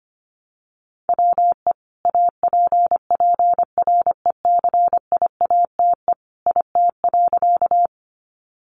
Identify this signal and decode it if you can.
Morse code, heard as a keyed tone, about 25 wpm, 700 Hz. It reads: WE APPRECIATE IT.
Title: Sentences from Top 1300 Words 25wpm